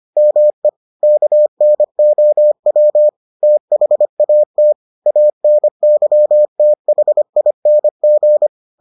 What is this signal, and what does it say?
Morse code, heard as a keyed tone, about 25 wpm, 600 Hz. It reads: ME KNOW THAT ANYTHING